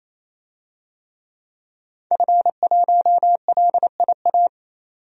Morse code, heard as a keyed tone, about 28 wpm, 700 Hz. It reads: F1LIA
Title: DXCC-100 Call Signs - 5 Characters - 28wpm